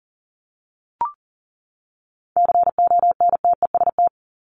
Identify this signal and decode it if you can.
Morse code, heard as a keyed tone, about 40 wpm, 700 Hz. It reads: CONTEST
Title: Top 1501-1750 Words 40wpm